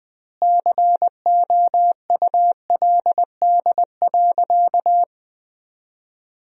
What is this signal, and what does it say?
Morse code, heard as a keyed tone, about 21 wpm, 700 Hz. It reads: COULD.
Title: 2x Word-Spacing - Sentences from Top 500 Words - 20wpm